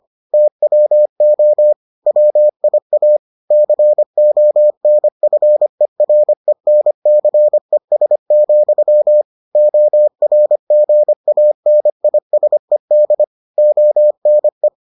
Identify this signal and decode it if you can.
Morse code, heard as a keyed tone, about 25 wpm, 600 Hz. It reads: TWO WIA CONFERENCES, ORGANISED ONE